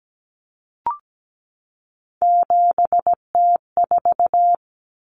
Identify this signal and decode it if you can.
Morse code, heard as a keyed tone, about 17 wpm, 700 Hz. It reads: 7T4
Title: Common Cut Numbers - 3-Digit 17wpm